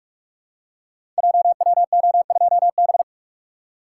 Morse code, heard as a keyed tone, about 44 wpm, 690 Hz. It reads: JWO2B
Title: Code Groups - 5 Letter:Numbers - 45wpm